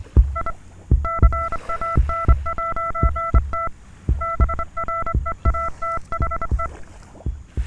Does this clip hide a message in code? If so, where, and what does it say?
Yes: Morse code from 0.4 to 6.7 s, heard as a keyed tone, about 24 wpm, 1340 Hz, with noise. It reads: I CAN'T BREATHE